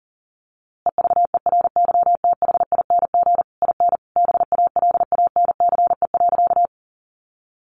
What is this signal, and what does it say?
Morse code, heard as a keyed tone, about 37 wpm, 700 Hz. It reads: EVERYTHING IN BALANCE.
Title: A Sound Of Thunder 40wpm